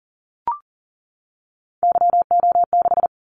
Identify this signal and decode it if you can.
Morse code, heard as a keyed tone, about 40 wpm, 700 Hz. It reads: YO6
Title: Callsign Prefixes for Top 25 DXCC - 40wpm